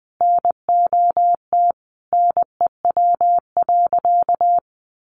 Morse code, heard as a keyed tone, about 20 wpm, 700 Hz. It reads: NOT NEW.
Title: Sentences from Top 1100 Words 20wpm